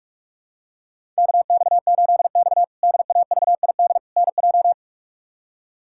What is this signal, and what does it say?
Morse code, heard as a keyed tone, about 46 wpm, 700 Hz. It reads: KX8X DAVID NJ
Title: K1USN SST QSO Practice - 45wpm